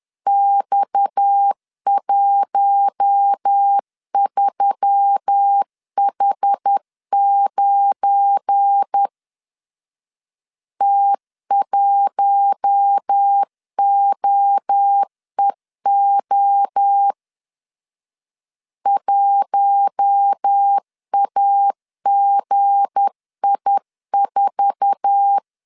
Morse code, heard as a keyed tone, about 10 wpm, 800 Hz. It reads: X13H9 T1OEO 1AGI4
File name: LEZIONI / 8 YJ45 / 8 YJ45 10wpm 2549